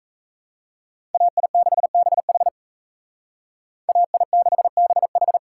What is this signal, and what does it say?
Morse code, heard as a keyed tone, about 36 wpm, 700 Hz. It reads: AI6BH AI6BH